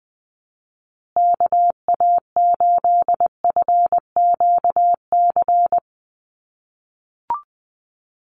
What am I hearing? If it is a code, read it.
Morse code, heard as a keyed tone, about 20 wpm, 700 Hz. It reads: KA8FQC
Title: USA Call Signs Repeated 3x 20wpm